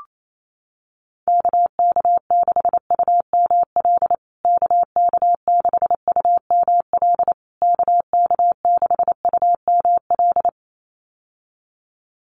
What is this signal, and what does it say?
Morse code, heard as a keyed tone, about 28 wpm, 700 Hz. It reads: KK6UML KK6UML KK6UML
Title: USA Call Signs Repeated 3x 28wpm